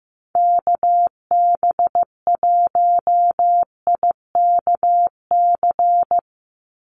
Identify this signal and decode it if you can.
Morse code, heard as a keyed tone, about 15 wpm, 700 Hz. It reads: KB1IKC
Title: USA Call Signs 15wpm